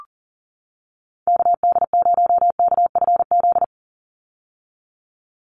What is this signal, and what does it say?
Morse code, heard as a keyed tone, about 40 wpm, 700 Hz. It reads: KD0KFZ
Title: USA Call Signs 40wpm